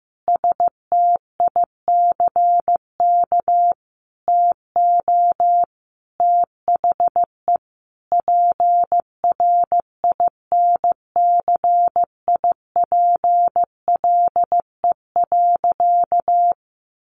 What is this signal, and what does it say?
Morse code, heard as a keyed tone, about 15 wpm, 700 Hz. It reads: STICK TO THE PRINCIPLE.
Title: Sentences from Top 1400 Words 15wpm